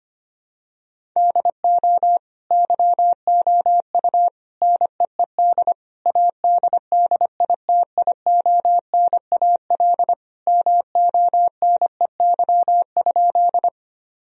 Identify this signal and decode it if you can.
Morse code, heard as a keyed tone, about 25 wpm, 700 Hz. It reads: DO YOU NEED ADDITIONAL MONEY?
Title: Sentences from Top 1000 Words - Review Part 4 - 25wpm